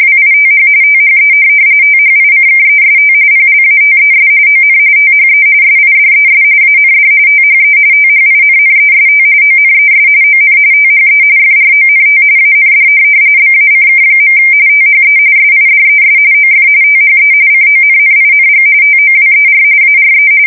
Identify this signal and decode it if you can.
RTTY, about 45.45 baud. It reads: WITH THE THICKEST PLATES ON THE SIDES AND THE THINNEST ONES ON THE FRONT AND BACK. THE SIDES OF THE CONNING TOWER ARE
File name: fsk170 / USS-Missouri